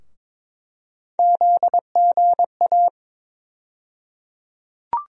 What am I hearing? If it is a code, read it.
Morse code, heard as a keyed tone, about 22 wpm, 700 Hz. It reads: ZGA
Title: USA Call Sign Postfixes - 3 Letters 22wpm